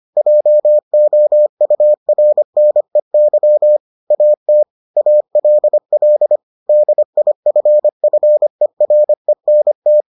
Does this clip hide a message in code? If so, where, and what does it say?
Yes: Morse code from 0.2 to 10.0 s, heard as a keyed tone, about 25 wpm, 600 Hz. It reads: JOURNEY AT ALL DIFFERENT